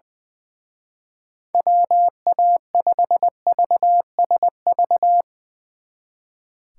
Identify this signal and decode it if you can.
Morse code, heard as a keyed tone, about 20 wpm, 700 Hz. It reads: WA5VSV
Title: Speed Racing - USA Call Signs 20wpm